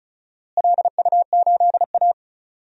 Morse code, heard as a keyed tone, about 34 wpm, 700 Hz. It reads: LU8A
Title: DXCC-100 Call Signs - 4 Characters - Repeated 3x - 35wpm